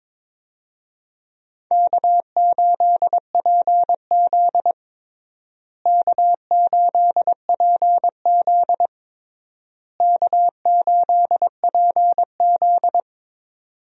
Morse code, heard as a keyed tone, about 22 wpm, 700 Hz. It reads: K8PZ K8PZ K8PZ
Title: USA Call Signs - 4 Characters - Repeated 3x - 22wpm